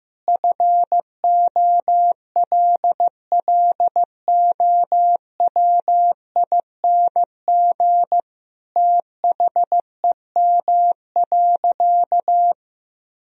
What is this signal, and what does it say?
Morse code, heard as a keyed tone, about 15 wpm, 700 Hz. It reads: FOLLOWING THEM.
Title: Sentences from Top 1000 Words - Review Part 4 - 15wpm